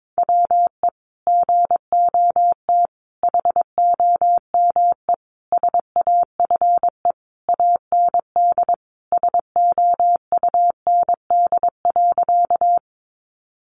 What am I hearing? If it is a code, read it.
Morse code, heard as a keyed tone, about 22 wpm, 700 Hz. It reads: WE GOT HOME SAFE AND SOUND.
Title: Sentences from Top 900 Words 22wpm